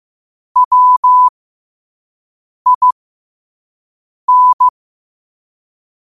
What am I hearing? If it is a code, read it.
Morse code, heard as a keyed tone, about 15 wpm, spaced as for 5 wpm, 1000 Hz. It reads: WIN
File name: qotdfiles / 5wpm